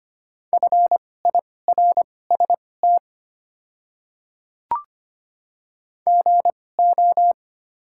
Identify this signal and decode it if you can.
Morse code, heard as a keyed tone, about 25 wpm, spaced as for 18 wpm, 690 Hz. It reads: FIRST GO
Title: 2x Character Spacing - Top 100 Words - 25wpm